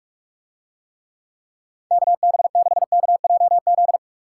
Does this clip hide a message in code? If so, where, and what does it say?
Yes: Morse code from 1.9 to 4.0 s, heard as a keyed tone, about 47 wpm, 700 Hz. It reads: KB6KJZ